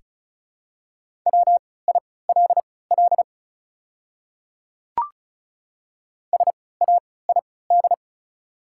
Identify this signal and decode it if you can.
Morse code, heard as a keyed tone, about 34 wpm, spaced as for 20 wpm, 690 Hz. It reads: WILL SAID
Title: 3x Character Spacing - Top 200 Words - 35wpm